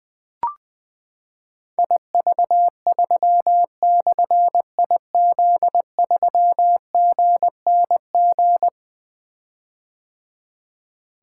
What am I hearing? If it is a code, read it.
Morse code, heard as a keyed tone, about 20 wpm, 700 Hz. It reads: IV3/IZ3GNG
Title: Difficult Call Signs 20wpm